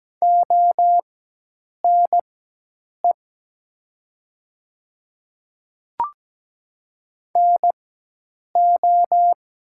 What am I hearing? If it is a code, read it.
Morse code, heard as a keyed tone, about 17 wpm, spaced as for 8 wpm, 700 Hz. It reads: ONE NO
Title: 4x Character Spacing - Top 300 Words - 17wpm